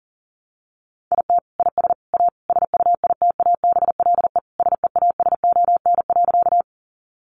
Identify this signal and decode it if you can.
Morse code, heard as a keyed tone, about 40 wpm, 700 Hz. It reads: IT IS A SUITABLE SEASON.
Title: Sentences from Top 1400 Words 40wpm